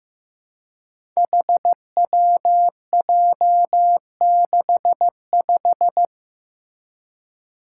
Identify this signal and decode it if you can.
Morse code, heard as a keyed tone, about 15 wpm, 700 Hz. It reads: HWJ65
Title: Five Challenging Letter-Number Combinations 15wpm